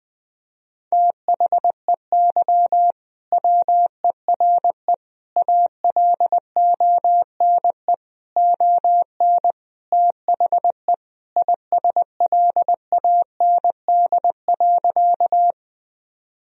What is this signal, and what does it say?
Morse code, heard as a keyed tone, about 20 wpm, 700 Hz. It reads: THEY WERE ALONE ON THE ISLAND.
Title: Sentences from Top 1000 Words - Review Part 3 - 20wpm